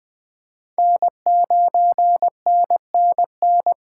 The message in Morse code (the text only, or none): N9NNN